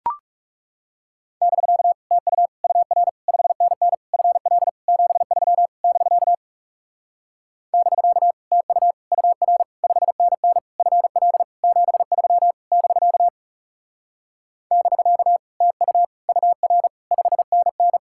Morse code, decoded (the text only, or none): <BK> TU UR 5NN FL 73 <BK> <BK> TU UR 5NN FL 73 <BK> <BK> TU UR 5NN